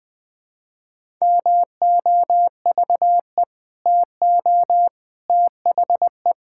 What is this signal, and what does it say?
Morse code, heard as a keyed tone, about 20 wpm, 700 Hz. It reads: MOVE TO THE
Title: Sentences with 5-Word Phrases - Spoken Beginning - 20wpm